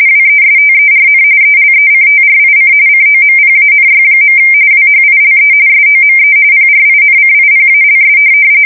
RTTY, about 45.45 baud. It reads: WESTERN ELECTRIC COMPANY. WHILE SOME PRINCIPALS
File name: fsk170 / teletype-corp